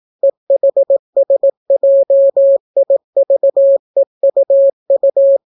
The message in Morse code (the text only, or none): EHSJIVEUU